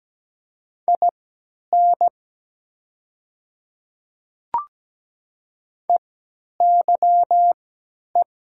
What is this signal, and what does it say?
Morse code, heard as a keyed tone, about 17 wpm, spaced as for 10 wpm, 700 Hz. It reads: IN EYE
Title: 3x Character Spacing - Top 300 Words - 17wpm